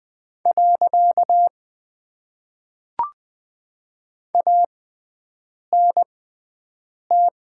Morse code, heard as a keyed tone, about 20 wpm, spaced as for 7 wpm, 700 Hz. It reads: . ANT
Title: Top 90 QSO Elements - 6x Character Spacing - 20wpm